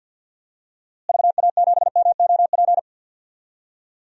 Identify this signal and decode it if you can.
Morse code, heard as a keyed tone, about 51 wpm, 700 Hz. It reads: VA7MOP
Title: DXCC-100 Call Signs - 6 Characters - Repeated 3x - 50wpm